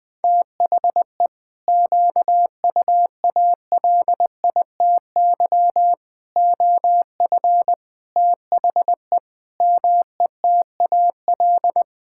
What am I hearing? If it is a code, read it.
Morse code, heard as a keyed tone, about 20 wpm, 700 Hz. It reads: THE QUALITY OF THE METAL
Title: Sentences from Top 1000 Words - Review Part 6 - 20wpm